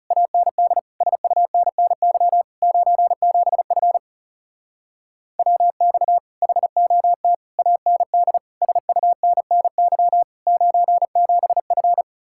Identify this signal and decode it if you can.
Morse code, heard as a keyed tone, about 40 wpm, 700 Hz. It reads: AND SUNNY 97F WX HOT AND SUNNY 97F